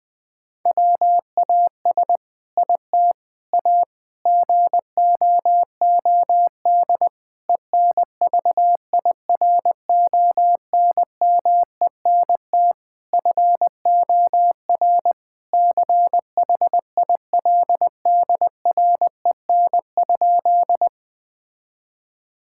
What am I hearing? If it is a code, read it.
Morse code, heard as a keyed tone, about 20 wpm, 700 Hz. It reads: WAS IT A GOOD ENVIRONMENT FOR CHILDREN?
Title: Sentences from Top 1000 Words - Review Part 5 - 20wpm